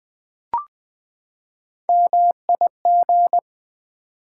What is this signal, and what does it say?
Morse code, heard as a keyed tone, about 20 wpm, 700 Hz. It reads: MIG
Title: Build A Word - 3 Letters At A Time - 9-Letter Words - 20wpm